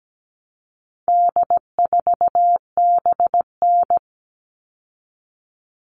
Morse code, heard as a keyed tone, about 17 wpm, 700 Hz. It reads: D4BN
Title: All Cut Numbers - 4-Digit 17wpm